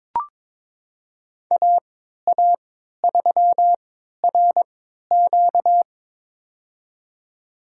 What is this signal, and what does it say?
Morse code, heard as a keyed tone, about 22 wpm, spaced as for 12 wpm, 700 Hz. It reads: AA3RQ
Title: USA Call Signs - 3x Character Spacing - 22wpm